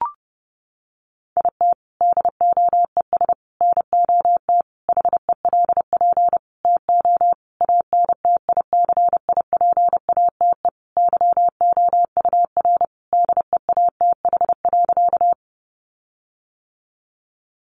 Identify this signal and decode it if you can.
Morse code, heard as a keyed tone, about 30 wpm, 700 Hz. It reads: IT DOES NOT HELP TO ANTICIPATE YOUR DEATH.
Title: Sentences from Top 1500 Words 30wpm